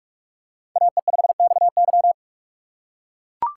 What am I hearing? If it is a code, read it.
Morse code, heard as a keyed tone, about 44 wpm, 700 Hz. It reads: AE5XY
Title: USA Call Signs - 6x Character Spacing - 45wpm